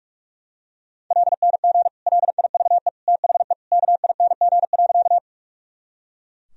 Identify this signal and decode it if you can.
Morse code, heard as a keyed tone, about 42 wpm, 700 Hz. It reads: LNG LIVE THE KING.
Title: Sentences from Top 400 Words and Abbreviations 45wpm